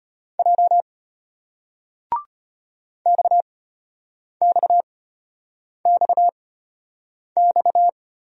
Morse code, heard as a keyed tone, about 35 wpm, spaced as for 8 wpm, 700 Hz. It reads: J XXXX